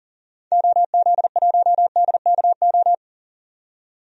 Morse code, heard as a keyed tone, about 39 wpm, 700 Hz. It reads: OZ1DKO
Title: DXCC-100 Call Signs - 6 Characters - 40wpm